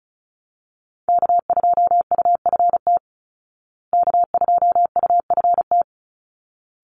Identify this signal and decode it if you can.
Morse code, heard as a keyed tone, about 34 wpm, 700 Hz. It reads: K2UFT K2UFT
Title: Call Signs Repeated 3x - 35wpm